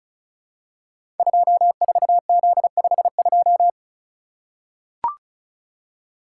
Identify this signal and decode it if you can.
Morse code, heard as a keyed tone, about 36 wpm, 700 Hz. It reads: 24Z52
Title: Code Groups - 5 Letter:Numbers - 35wpm